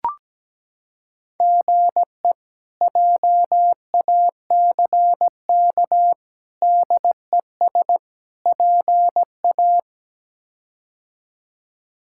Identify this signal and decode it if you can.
Morse code, heard as a keyed tone, about 17 wpm, 700 Hz. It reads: GE JACK DES PA
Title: K1USN SST QSO Practice - 17wpm